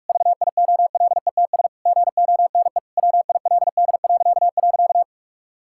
Morse code, heard as a keyed tone, about 46 wpm, 700 Hz. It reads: VIOLETS GONE WILD..
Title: Pangram Challenge v2 - 45wpm